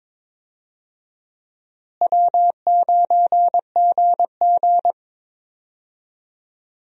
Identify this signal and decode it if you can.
Morse code, heard as a keyed tone, about 22 wpm, 700 Hz. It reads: W9GG
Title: USA Call Signs - 4 Characters - Repeated 3x - 22wpm